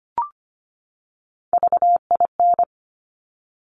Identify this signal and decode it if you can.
Morse code, heard as a keyed tone, about 25 wpm, 700 Hz. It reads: VIN
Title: Top Three Letter Combinations in Words 25wpm